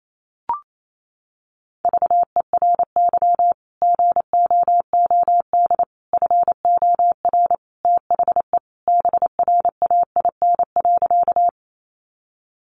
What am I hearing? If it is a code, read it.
Morse code, heard as a keyed tone, about 28 wpm, 700 Hz. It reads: VERY GOOD FOR THE BRAIN.
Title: Sentences from Top 1100 Words 28wpm